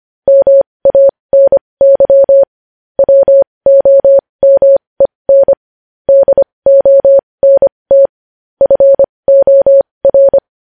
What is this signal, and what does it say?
Morse code, heard as a keyed tone, about 25 wpm, spaced as for 20 wpm, 550 Hz. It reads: MANY WOMEN DONT FOR
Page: news-headlines-in-morse-code-at-20-wpm